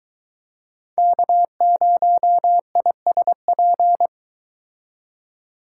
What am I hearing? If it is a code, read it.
Morse code, heard as a keyed tone, about 22 wpm, 700 Hz. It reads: K0ISP